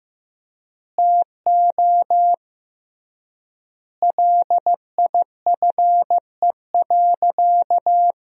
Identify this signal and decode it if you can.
Morse code, heard as a keyed tone, about 15 wpm, 700 Hz. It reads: TO LIFE.
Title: 3x Word-Spacing - Sentences from Top 400 Words - 15wpm